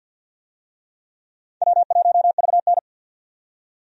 Morse code, heard as a keyed tone, about 52 wpm, 700 Hz. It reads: W1VN
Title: USA Call Signs - 4 Characters - 50wpm